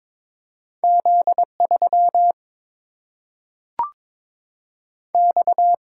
Morse code, heard as a keyed tone, about 22 wpm, 700 Hz. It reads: Z 3 X
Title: Single-Letter Single-Number 22wpm